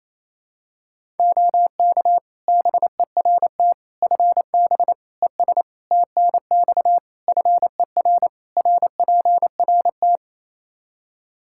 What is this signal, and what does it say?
Morse code, heard as a keyed tone, about 28 wpm, 700 Hz. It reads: OK BERT FB ES TNX FER RPRT
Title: Practice QSO - Exchange 2 - Info - 28wpm